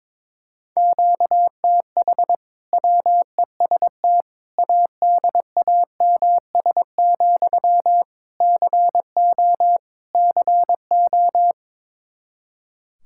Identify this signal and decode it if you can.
Morse code, heard as a keyed tone, about 22 wpm, 700 Hz. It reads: QTH WEST ADAMS, CO CO
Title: Practice QSO - Exchange 1 - Beginning - 22wpm